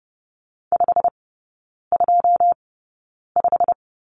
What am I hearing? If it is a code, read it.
Morse code, heard as a keyed tone, about 30 wpm, spaced as for 9 wpm, 700 Hz. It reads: 525
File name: course / 30:10wpm / 087 Lesson 15e - Number 5 - Three Characters 30wpm